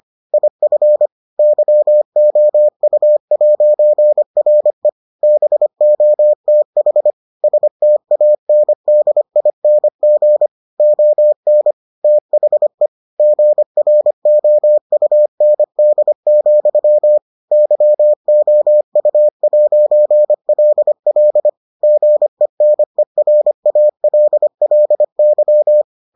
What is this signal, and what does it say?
Morse code, heard as a keyed tone, about 25 wpm, 600 Hz. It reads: IF YOU'RE BOTH STANDING ON THE GROUND, YOU'LL GENERALLY